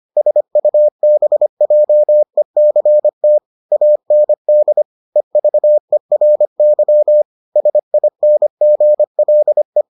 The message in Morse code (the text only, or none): SUBJECT AND EVERY SINGLE